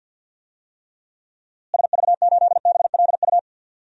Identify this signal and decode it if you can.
Morse code, heard as a keyed tone, about 47 wpm, 700 Hz. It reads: SV8BLU
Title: DXCC-100 Call Signs - 6 Characters - Repeated 2x - 50wpm